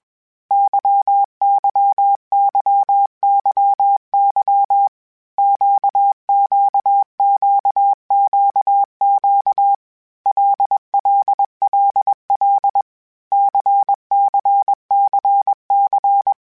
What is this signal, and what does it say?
Morse code, heard as a keyed tone, about 21 wpm, 800 Hz. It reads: YYYYY QQQQQ LLLL CCCC
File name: Basic Sets / Basic Letter Groups Collective7 / LFCYQ@20